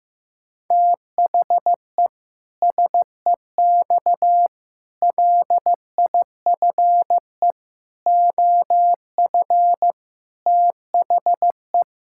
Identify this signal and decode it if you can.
Morse code, heard as a keyed tone, about 16 wpm, 700 Hz. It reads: THE SEX LIFE OF THE
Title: Pangram Challenge v2 - 15wpm